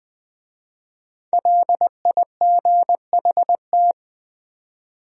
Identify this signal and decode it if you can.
Morse code, heard as a keyed tone, about 24 wpm, 700 Hz. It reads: LIGHT
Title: Speed-Racing Top 200 Words 20wpm